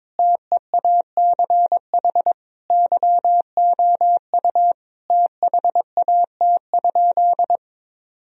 Morse code, heard as a keyed tone, about 22 wpm, 700 Hz. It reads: TEACH YOU THAT?
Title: Sentences from Top 1000 Words - Review Part 2 - 22wpm